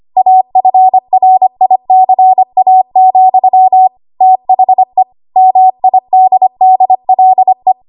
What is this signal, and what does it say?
Morse code, heard as a keyed tone, about 25 wpm, 750 Hz. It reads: AFRICA, THE MIDDLE
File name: Archive / 25 WPM / 25WPM